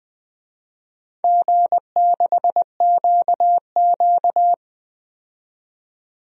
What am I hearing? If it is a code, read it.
Morse code, heard as a keyed tone, about 20 wpm, 700 Hz. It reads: G6QQ
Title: DXCC-100 Call Signs - 4 Characters - 20wpm